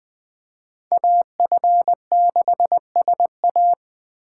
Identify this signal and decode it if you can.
Morse code, heard as a keyed tone, about 20 wpm, 700 Hz. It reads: AF6SA
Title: Speed Racing - USA 2x2 Call Signs 17wpm